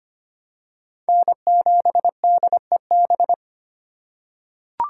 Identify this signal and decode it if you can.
Morse code, heard as a keyed tone, about 25 wpm, 700 Hz. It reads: N7DEB
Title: USA Call Signs - 5 Characters - 3x Character Spacing - 25wpm